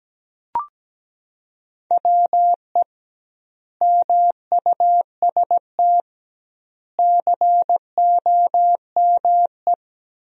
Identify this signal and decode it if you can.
Morse code, heard as a keyed tone, about 17 wpm, 700 Hz. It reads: WE MUST COME
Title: 2x Word-Spacing - Sentences from Top 200 Words - 17wpm